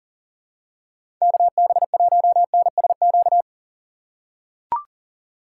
Morse code, heard as a keyed tone, about 40 wpm, 700 Hz. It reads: KB1NSQ
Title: USA Call Signs - 6 Characters - 4x Character Spacing - 40wpm